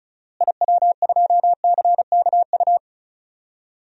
Morse code, heard as a keyed tone, about 34 wpm, 700 Hz. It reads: IW2CKU
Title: DXCC-100 Call Signs - 6 Characters - Repeated 2x - 35wpm